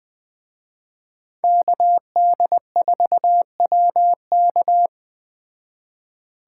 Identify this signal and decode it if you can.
Morse code, heard as a keyed tone, about 20 wpm, 700 Hz. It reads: KD4WK